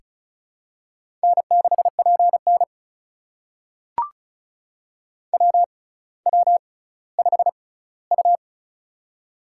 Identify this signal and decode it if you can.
Morse code, heard as a keyed tone, about 36 wpm, spaced as for 18 wpm, 700 Hz. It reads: N6PN WW5U
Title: USA Call Signs - 4 Characters - 6x Character Spacing - 35wpm